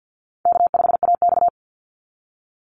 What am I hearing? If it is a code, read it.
Morse code, heard as a keyed tone, about 50 wpm, 700 Hz. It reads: K5AX
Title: Call Signs - 50wpm